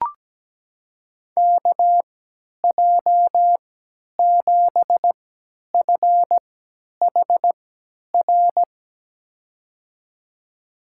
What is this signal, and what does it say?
Morse code, heard as a keyed tone, about 17 wpm, spaced as for 10 wpm, 700 Hz. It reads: KJ7FHR
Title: USA Call Signs - 6 Characters - 3x Character Spacing - 17wpm